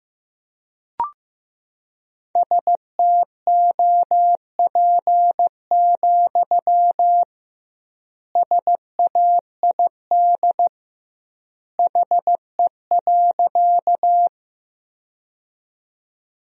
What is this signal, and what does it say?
Morse code, heard as a keyed tone, about 16 wpm, 700 Hz. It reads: STOP, SAID HE.
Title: 2x Word-Spacing - Sentences from Top 300 Words - 15wpm